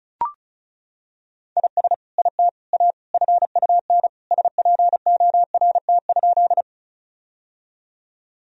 Morse code, heard as a keyed tone, about 36 wpm, 700 Hz. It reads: IS IT A FUN SPORT?
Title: Sentences from Top 1000 Words - Review Part 4 - 35wpm